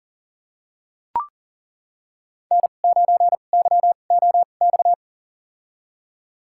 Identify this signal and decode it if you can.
Morse code, heard as a keyed tone, about 37 wpm, spaced as for 28 wpm, 700 Hz. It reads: N9YOX
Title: USA Call Signs - 5 Characters - 2x Character Spacing - 40wpm